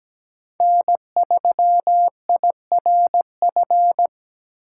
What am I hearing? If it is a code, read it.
Morse code, heard as a keyed tone, about 17 wpm, 700 Hz. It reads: N3IRF
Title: USA Call Signs - 8x Character Spacing - 17wpm